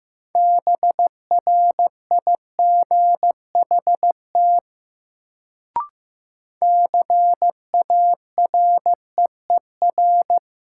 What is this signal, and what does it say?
Morse code, heard as a keyed tone, about 15 wpm, 700 Hz. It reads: BRIGHT CAREER
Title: Sentences from Top 1000 Words - New Word Drill - 15wpm